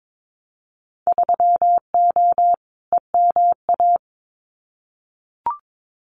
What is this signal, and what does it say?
Morse code, heard as a keyed tone, about 22 wpm, 700 Hz. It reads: 3O EMA
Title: Winter Field Day Exchanges Encore 22wpm